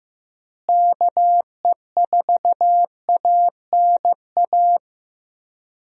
Morse code, heard as a keyed tone, about 15 wpm, 700 Hz. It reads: KE4ANA
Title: USA Call Signs - 6 Characters - Repeated 3x - 15wpm